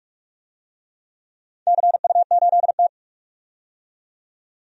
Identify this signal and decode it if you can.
Morse code, heard as a keyed tone, about 45 wpm, 700 Hz. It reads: CU8T
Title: DXCC-100 Call Signs - 4 Characters - 45wpm